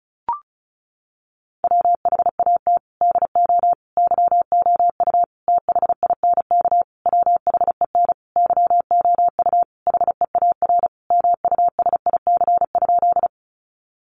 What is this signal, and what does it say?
Morse code, heard as a keyed tone, about 34 wpm, 700 Hz. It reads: WHAT DO YOU THINK WHEN YOU HEAR MUSIC?
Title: Sentences from Top 300 Words 35wpm